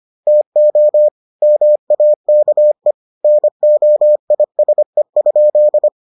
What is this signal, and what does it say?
Morse code, heard as a keyed tone, about 25 wpm, 600 Hz. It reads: TO MAKE NOISE?